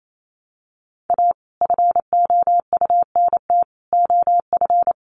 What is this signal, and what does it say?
Morse code, heard as a keyed tone, about 28 wpm, 700 Hz. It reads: A FOUNT OF
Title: A Sound Of Thunder 28wpm